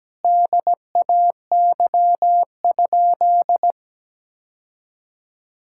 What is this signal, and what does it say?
Morse code, heard as a keyed tone, about 17 wpm, 700 Hz. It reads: DAY?
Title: Sentences from Top 1000 Words - Review Part 6 - 17wpm